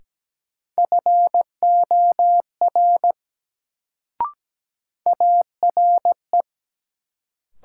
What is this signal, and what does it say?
Morse code, heard as a keyed tone, about 17 wpm, 700 Hz. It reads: FOR ARE
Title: Sentences from Top 100 Words - New Word Drill - 17wpm